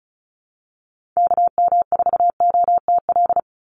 Morse code, heard as a keyed tone, about 34 wpm, 700 Hz. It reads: KM4OTL
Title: USA Call Signs 35wpm